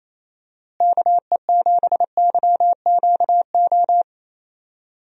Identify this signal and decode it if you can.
Morse code, heard as a keyed tone, about 27 wpm, 700 Hz. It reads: KE7YQO
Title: USA Call Signs - 6 Characters - Repeated 2x - 28wpm